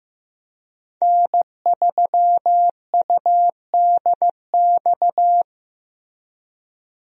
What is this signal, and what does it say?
Morse code, heard as a keyed tone, about 15 wpm, 700 Hz. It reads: N3UDX
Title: USA Call Signs - 5 Characters - Repeated 3x - 15wpm